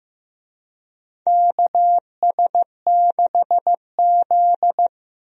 Morse code, heard as a keyed tone, about 15 wpm, 700 Hz. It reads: KS6Z
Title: USA Call Signs - 4 Characters - Repeated 2x - 15wpm